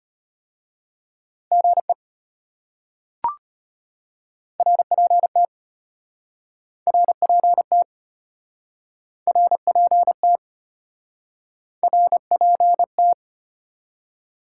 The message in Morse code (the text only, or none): GE RPT RPT RPT RPT